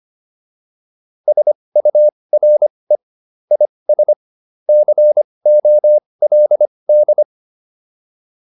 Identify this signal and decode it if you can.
Morse code, heard as a keyed tone, about 25 wpm, spaced as for 20 wpm, 600 Hz. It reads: SURE IS COLD
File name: PR303 20